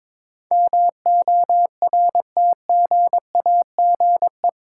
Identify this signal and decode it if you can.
Morse code, heard as a keyed tone, about 22 wpm, 700 Hz. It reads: MORTGAGE
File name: practice / 22wpm / Eight Letter Words - 22wpm